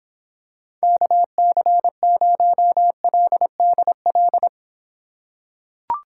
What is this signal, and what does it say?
Morse code, heard as a keyed tone, about 23 wpm, 700 Hz. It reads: KC0LDL